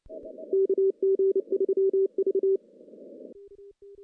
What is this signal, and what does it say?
Morse code, heard as a keyed tone, about 29 wpm, 390 Hz. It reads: KG3V
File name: contests / CW-SS-k5tr- -0100